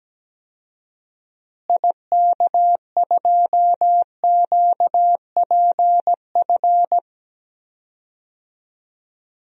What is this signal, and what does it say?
Morse code, heard as a keyed tone, about 17 wpm, 700 Hz. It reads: IK2QPF
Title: DXCC-100 Call Signs - 6 Characters - Repeated 2x - 17wpm